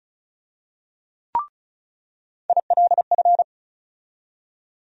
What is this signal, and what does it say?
Morse code, heard as a keyed tone, about 36 wpm, 700 Hz. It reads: ILF
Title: USA Call Sign Postfixes - 3 Letters 35wpm